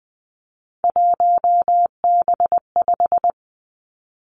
Morse code, heard as a keyed tone, about 20 wpm, 700 Hz. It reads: 1B5
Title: Three Challenging Letter-Number Combinations 20wpm